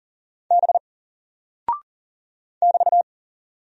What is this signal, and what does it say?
Morse code, heard as a keyed tone, about 40 wpm, 700 Hz. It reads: B <BT>
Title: Single Character (Letter:Number:Punctuation:Prosign) 40wpm